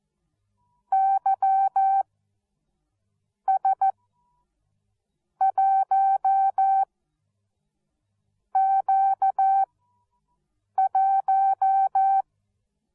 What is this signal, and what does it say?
Morse code, heard as a keyed tone, about 14 wpm, spaced as for 5 wpm, 770 Hz. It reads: YS1Q1
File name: Onlinelehrgang / morsen / DJ4UF Morsen13